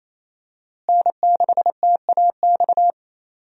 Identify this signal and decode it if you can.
Morse code, heard as a keyed tone, about 27 wpm, 700 Hz. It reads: N6TAX
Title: USA Call Signs - 5 Characters - Repeated 3x - 28wpm